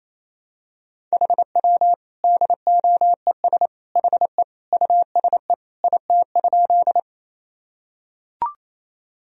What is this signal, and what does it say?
Morse code, heard as a keyed tone, about 28 wpm, 700 Hz. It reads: HW DOES HE USE IT?
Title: Sentences from Top 200 Words and Abbreviations 28wpm